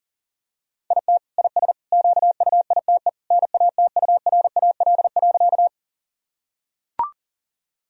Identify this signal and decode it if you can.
Morse code, heard as a keyed tone, about 40 wpm, 700 Hz. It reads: IT IS QUITE NATURAL.
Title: Sentences from Top 1000 Words - Review Part 4 - 40wpm